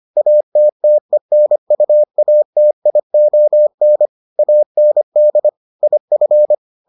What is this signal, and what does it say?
Morse code, heard as a keyed tone, about 25 wpm, 600 Hz. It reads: ATTENUATION AND IF